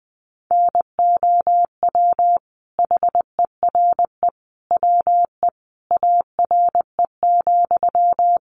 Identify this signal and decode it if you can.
Morse code, heard as a keyed tone, about 20 wpm, 700 Hz. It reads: NOW HERE WE ARE,
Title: Sentences from Top 1200 Words 20wpm